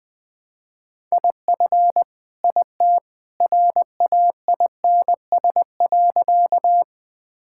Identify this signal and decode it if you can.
Morse code, heard as a keyed tone, about 20 wpm, 700 Hz. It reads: IF IT RAINS.
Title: Sentences with 2-Word Phrases - Spoken Beginning - 20wpm